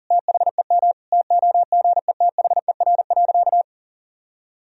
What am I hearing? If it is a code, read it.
Morse code, heard as a keyed tone, about 40 wpm, 700 Hz. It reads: THEM TOGETHER.
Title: Sentences from Top 1000 Words - Review Part 3 - 40wpm